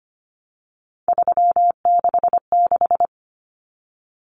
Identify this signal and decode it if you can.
Morse code, heard as a keyed tone, about 25 wpm, 700 Hz. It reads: 366
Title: Common Cut Numbers - 3-Digit 25wpm